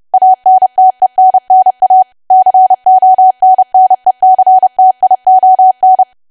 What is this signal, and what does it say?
Morse code, heard as a keyed tone, about 30 wpm, 750 Hz. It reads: ANTENNA CONNECTION
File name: cwa / W1AW-Level3 / 30WPM